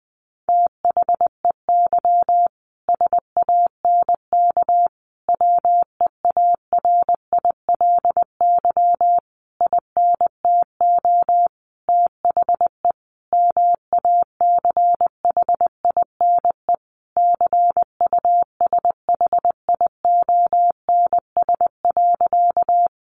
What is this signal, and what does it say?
Morse code, heard as a keyed tone, about 20 wpm, 700 Hz. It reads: THEY SANK WEARILY INTO THE MACHINE CUSHIONS.
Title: A Sound Of Thunder 20wpm